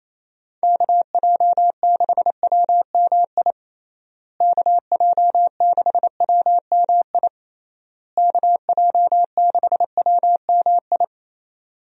Morse code, heard as a keyed tone, about 28 wpm, 700 Hz. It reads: KJ6WMI KJ6WMI KJ6WMI
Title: USA Call Signs - 6 Characters - Repeated 3x - 28wpm